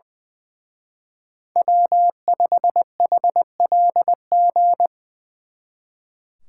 Morse code, heard as a keyed tone, about 24 wpm, 700 Hz. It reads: W5HLG
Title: Speed Racing - USA 1x3 Call Signs 20wpm